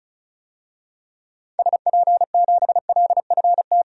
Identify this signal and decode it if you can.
Morse code, heard as a keyed tone, about 36 wpm, 700 Hz. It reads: SP7LFT
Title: DXCC-100 Call Signs - 6 Characters - 35wpm